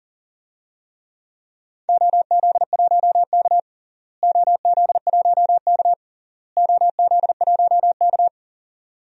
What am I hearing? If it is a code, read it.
Morse code, heard as a keyed tone, about 37 wpm, 700 Hz. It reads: OZ1K OZ1K OZ1K
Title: DXCC-100 Call Signs - 4 Characters - Repeated 3x - 40wpm